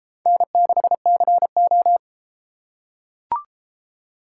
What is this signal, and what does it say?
Morse code, heard as a keyed tone, about 30 wpm, 700 Hz. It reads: N6CO